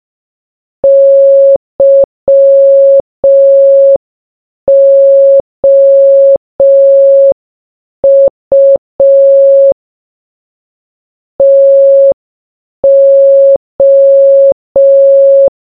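Morse code, heard as a keyed tone, about 5 wpm, 550 Hz. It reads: YOU TO
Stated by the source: Morse sound from the game Baba Is You.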